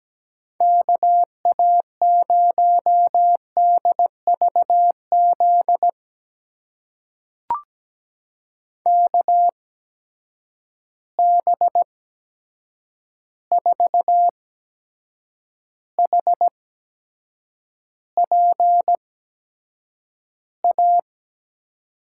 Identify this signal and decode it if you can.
Morse code, heard as a keyed tone, about 17 wpm, spaced as for 7 wpm, 700 Hz. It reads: KA0DVZ KB4HPA
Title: USA Call Signs - 6 Characters - 8x Character Spacing - 17wpm